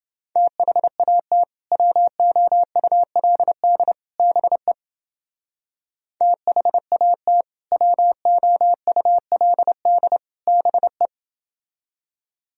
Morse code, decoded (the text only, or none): THAT WOULD BE THAT WOULD BE